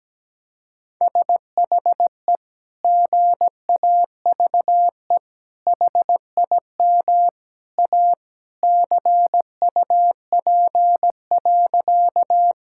Morse code, heard as a keyed tone, about 17 wpm, 700 Hz. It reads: SHE GAVE HIM A CUP.
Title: Sentences from Top 1600 Words 17wpm